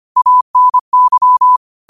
Morse code, heard as a keyed tone, about 25 wpm, 1000 Hz. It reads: ANY
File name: qotdfiles / 25wpm